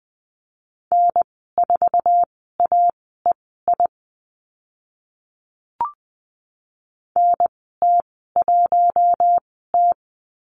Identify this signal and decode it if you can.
Morse code, heard as a keyed tone, about 20 wpm, spaced as for 14 wpm, 700 Hz. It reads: N4AEI NT1T
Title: 048 Lesson 9g - Number 4 - Call Signs 20wpm